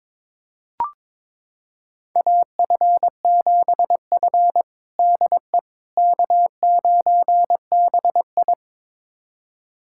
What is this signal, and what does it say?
Morse code, heard as a keyed tone, about 22 wpm, 700 Hz. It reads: AF7F DE K9BI
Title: USA Callsign DE Callsign - 4 Characters - 22wpm